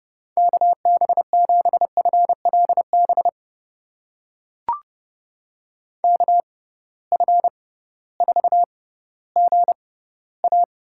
Morse code, heard as a keyed tone, about 30 wpm, spaced as for 16 wpm, 700 Hz. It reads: KB7FLB KF4GA
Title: USA Call Signs - 6 Characters - 6x Character Spacing - 30wpm